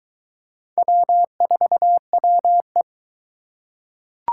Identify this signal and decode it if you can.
Morse code, heard as a keyed tone, about 22 wpm, 700 Hz. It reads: W4WE